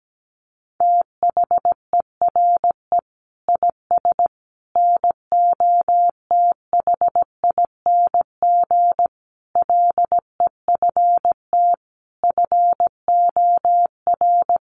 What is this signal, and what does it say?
Morse code, heard as a keyed tone, about 17 wpm, 700 Hz. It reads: THERE IS NOTHING LEFT FOR
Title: Sentences from Top 1100 Words 17wpm